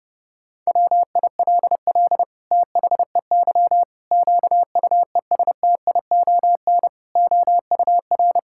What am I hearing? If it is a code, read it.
Morse code, heard as a keyed tone, about 30 wpm, 700 Hz. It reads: WILL THEY QUESTION OUR
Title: Sentences from Top 1000 Words - Review Part 5 - 30wpm